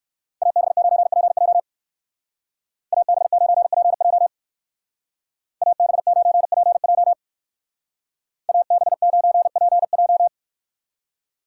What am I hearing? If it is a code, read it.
Morse code, heard as a keyed tone, about 62 wpm, 700 Hz. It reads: AB9PJ AB9PJ AB9PJ AB9PJ